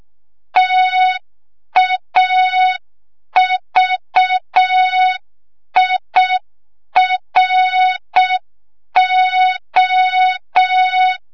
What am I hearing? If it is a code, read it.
Morse code, heard as a keyed tone, about 6 wpm, 3700 Hz. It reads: TAVIRO
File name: Morze-HA2MN / hang / 25 GYAKORLAT